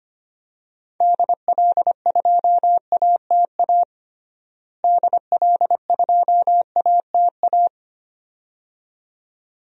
Morse code, heard as a keyed tone, about 25 wpm, 700 Hz. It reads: DL2ATA DL2ATA
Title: DXCC-100 Call Signs - 6 Characters - Repeated 2x - 25wpm